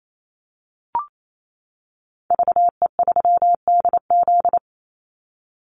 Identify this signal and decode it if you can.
Morse code, heard as a keyed tone, about 28 wpm, 700 Hz. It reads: VE3DZ
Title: Call Signs - 28wpm